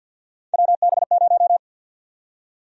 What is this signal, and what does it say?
Morse code, heard as a keyed tone, about 47 wpm, 700 Hz. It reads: WB0
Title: USA Call Sign Prefixes - 50wpm